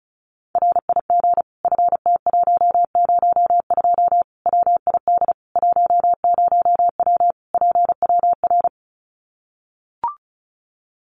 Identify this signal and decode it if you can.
Morse code, heard as a keyed tone, about 34 wpm, 700 Hz. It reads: RIG FT102 WID 10W PWR
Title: Common QSO Phrases 35wpm